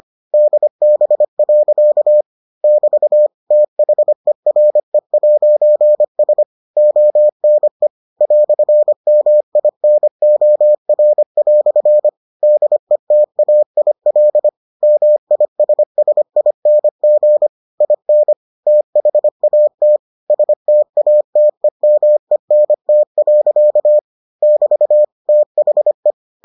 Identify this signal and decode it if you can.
Morse code, heard as a keyed tone, about 25 wpm, 600 Hz. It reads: DB. <BT> THERE'S ONE "MINOR" DETAIL MISSING IN THAT STATEMENT. <BT> THE